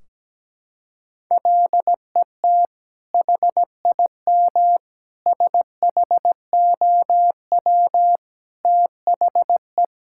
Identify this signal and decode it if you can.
Morse code, heard as a keyed tone, about 17 wpm, 700 Hz. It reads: LET HIM SHOW THE
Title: Sentences from Top 1600 Words 17wpm